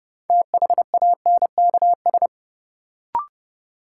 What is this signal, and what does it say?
Morse code, heard as a keyed tone, about 30 wpm, 700 Hz. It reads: THANKS
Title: Sets of 5 Words - New Word Drill - 30wpm